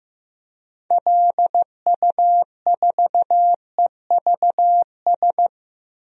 Morse code, heard as a keyed tone, about 15 wpm, 700 Hz. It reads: LU4EVS
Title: DXCC-100 Call Signs - 6 Characters - 15wpm